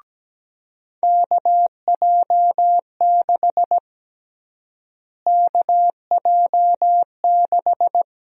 Morse code, heard as a keyed tone, about 17 wpm, 700 Hz. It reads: KJ6 KJ6
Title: USA Call Sign Prefixes - Repeated 2x - 17wpm